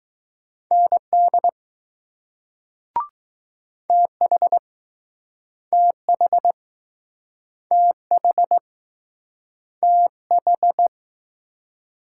Morse code, heard as a keyed tone, about 20 wpm, 700 Hz. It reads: ND TH TH TH TH